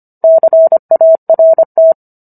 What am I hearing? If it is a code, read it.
Morse code, heard as a keyed tone, about 25 wpm, 650 Hz. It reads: CART